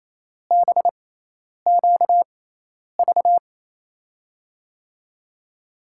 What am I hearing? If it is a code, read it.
Morse code, heard as a keyed tone, about 28 wpm, spaced as for 10 wpm, 700 Hz. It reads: BQV
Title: USA Call Signs - 6 Characters - 6x Character Spacing - 28wpm